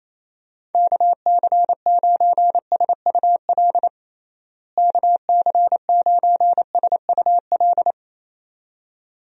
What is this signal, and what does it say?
Morse code, heard as a keyed tone, about 27 wpm, 700 Hz. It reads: KC9SUL KC9SUL
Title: USA Call Signs - 6 Characters - Repeated 2x - 28wpm